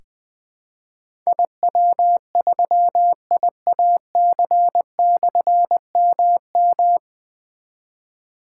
Morse code, heard as a keyed tone, about 20 wpm, 700 Hz. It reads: IW3IAC/MM
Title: Difficult Call Signs 20wpm